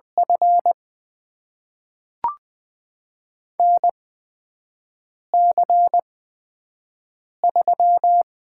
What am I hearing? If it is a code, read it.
Morse code, heard as a keyed tone, about 20 wpm, spaced as for 5 wpm, 700 Hz. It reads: F NC3
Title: USA Call Signs - 4 Characters - 8x Character Spacing - 20wpm